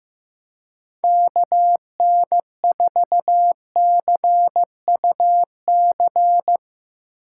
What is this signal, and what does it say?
Morse code, heard as a keyed tone, about 16 wpm, 700 Hz. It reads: KN4CUC
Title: Speed Racing - USA 2x3 Call Signs 15wpm